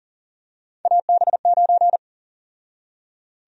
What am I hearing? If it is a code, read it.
Morse code, heard as a keyed tone, about 40 wpm, 700 Hz. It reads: AB9
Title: USA Call Sign Prefixes - 40wpm